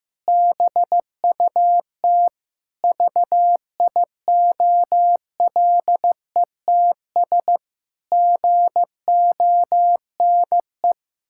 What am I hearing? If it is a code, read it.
Morse code, heard as a keyed tone, about 15 wpm, 700 Hz. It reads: BUT VIOLETS GONE
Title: Pangram Challenge v2 - 15wpm